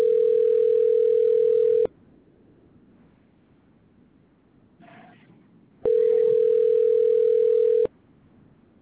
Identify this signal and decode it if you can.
Ringback tone (North America), 440 + 480 Hz.